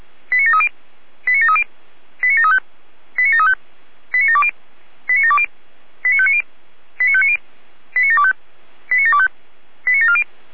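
Five-tone selective call (ZVEI), heard as quick runs of five tones, tones of about 70 ms. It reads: 89730 89730 89735 89735 89720 89720 89590 89590 89725 89725 89740
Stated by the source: SDR-received ZVEI five-tone selective call, code 89705.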